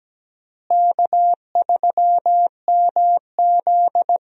Morse code, heard as a keyed tone, about 17 wpm, 700 Hz. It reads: K3MZ
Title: USA Call Signs - 4 Characters - Repeated 2x - 17wpm